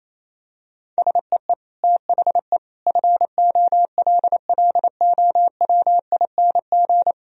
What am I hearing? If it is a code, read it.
Morse code, heard as a keyed tone, about 28 wpm, 700 Hz. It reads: SEE THE FOLLOWING
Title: Sentences from Top 1000 Words - Review Part 1 - 28wpm